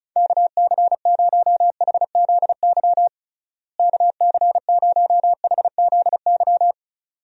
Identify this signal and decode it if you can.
Morse code, heard as a keyed tone, about 34 wpm, 700 Hz. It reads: KC0HZY KC0HZY
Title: USA Call Signs - 6 Characters - Repeated 2x - 35wpm